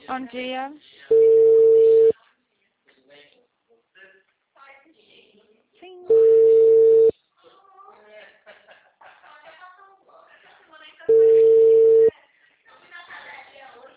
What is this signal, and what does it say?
Ringback tone (China, 450 Hz), with speech.
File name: out- -2004- - - .0